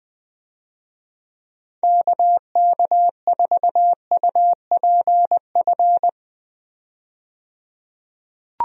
Morse code, heard as a keyed tone, about 20 wpm, 700 Hz. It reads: KK4UPF
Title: USA Call Signs - 6 Characters - 20wpm